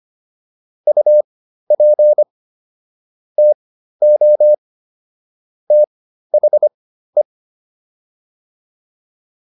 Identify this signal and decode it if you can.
Morse code, heard as a keyed tone, about 25 wpm, spaced as for 13 wpm, 600 Hz. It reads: UP TO THE